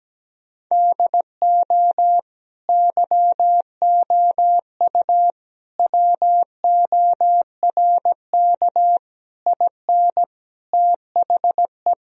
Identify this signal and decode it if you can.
Morse code, heard as a keyed tone, about 17 wpm, 700 Hz. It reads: DO YOU WORK IN THE
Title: Sentences from Top 1000 Words - Review Part 4 - 17wpm